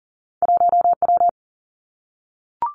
Morse code, heard as a keyed tone, about 39 wpm, 700 Hz. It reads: 1W
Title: Two Challenging Letter-Number Combinations 40wpm